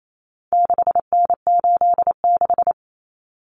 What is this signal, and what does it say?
Morse code, heard as a keyed tone, about 28 wpm, 700 Hz. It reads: 6N86
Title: Common Cut Numbers - 4-Digit 28wpm